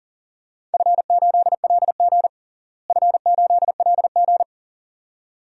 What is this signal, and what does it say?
Morse code, heard as a keyed tone, about 40 wpm, 700 Hz. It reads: F8LG F8LG
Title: DXCC-100 Call Signs - 4 Characters - Repeated 2x - 40wpm